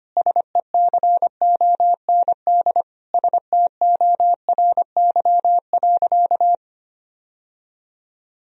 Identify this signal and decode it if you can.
Morse code, heard as a keyed tone, about 25 wpm, 700 Hz. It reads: SECOND STORY.
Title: Sentences from Top 1000 Words - Review Part 1 - 25wpm